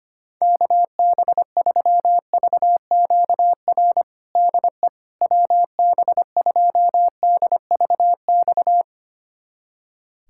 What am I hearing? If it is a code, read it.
Morse code, heard as a keyed tone, about 25 wpm, 700 Hz. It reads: KB3VQR DE WB2DVX
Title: USA Callsign DE Callsign - 6 Characters - 25wpm